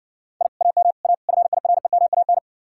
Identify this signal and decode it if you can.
Morse code, heard as a keyed tone, about 60 wpm, 700 Hz. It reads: I AM A FIREMAN